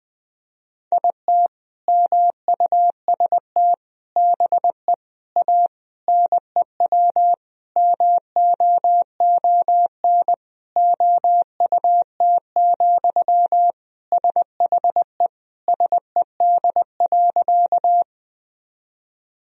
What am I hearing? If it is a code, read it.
Morse code, heard as a keyed tone, about 20 wpm, 700 Hz. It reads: IT MUST BE A NEW MOON OUT, SHE SED.
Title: Sentences from Top 500 Words and Abbreviations 20wpm